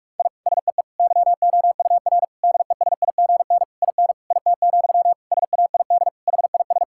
Morse code, heard as a keyed tone, about 46 wpm, 700 Hz. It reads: I SEE YOUR DESIGN IN IT, SAID HIS
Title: Sentences from Top 1000 Words - Review Part 5 - 45wpm